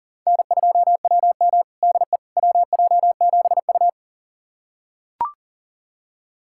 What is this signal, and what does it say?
Morse code, heard as a keyed tone, about 40 wpm, 700 Hz. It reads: N2WM DE WJ7U
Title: USA Callsign DE Callsign - 4 Characters - 40wpm